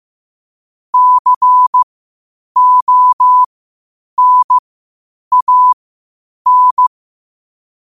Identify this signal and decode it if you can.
Morse code, heard as a keyed tone, about 15 wpm, spaced as for 8 wpm, 1000 Hz. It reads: CONAN
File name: qotdfiles / 8wpm